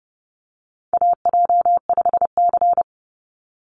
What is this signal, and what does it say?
Morse code, heard as a keyed tone, about 30 wpm, 700 Hz. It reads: AJ5C
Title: NA SOTA Chaser Callsigns 30wpm